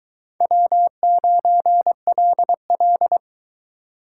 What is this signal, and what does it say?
Morse code, heard as a keyed tone, about 22 wpm, 700 Hz. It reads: W9LL